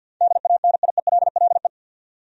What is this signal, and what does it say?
Morse code, heard as a keyed tone, about 46 wpm, 700 Hz. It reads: DANIELLE
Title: USA Common Names - 50wpm